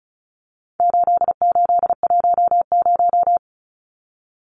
Morse code, heard as a keyed tone, about 34 wpm, 700 Hz. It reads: 8810
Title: Numbers - 4-Digit 35wpm